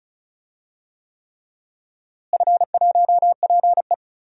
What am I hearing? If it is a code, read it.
Morse code, heard as a keyed tone, about 34 wpm, 700 Hz. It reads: F1PE
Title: DXCC-100 Call Signs - 4 Characters - Repeated 3x - 35wpm